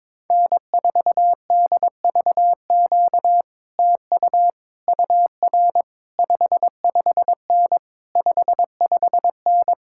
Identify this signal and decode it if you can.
Morse code, heard as a keyed tone, about 22 wpm, 700 Hz. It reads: N4DVQ TU UR 55N 55N
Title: Speed-Racing Common QSO Phrases - 22wpm